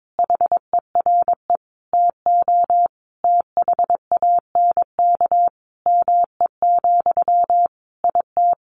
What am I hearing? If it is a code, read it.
Morse code, heard as a keyed tone, about 22 wpm, 700 Hz. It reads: HERE TO THANK ME, IT
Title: Sentences from Top 800 Words 22wpm